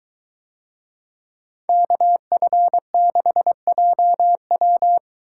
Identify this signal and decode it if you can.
Morse code, heard as a keyed tone, about 22 wpm, 700 Hz. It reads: KF6JW